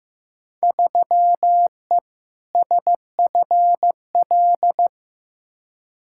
Morse code, heard as a keyed tone, about 15 wpm, 700 Hz. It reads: 3E SFL
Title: Field Day Exchanges Encore - 15wpm